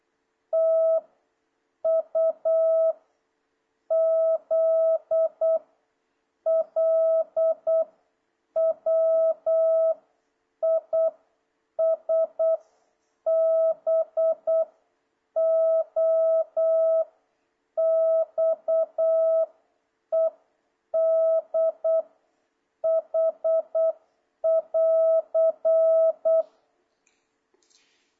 Morse code, heard as a keyed tone, about 8 wpm, 650 Hz. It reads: T U Z L W I S B O X E D H <AR>
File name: jh8cbh / 2026cw